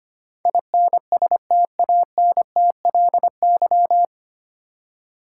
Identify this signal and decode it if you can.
Morse code, heard as a keyed tone, about 25 wpm, 700 Hz. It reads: INSTANTLY
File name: Nine Letter Words - 25wpm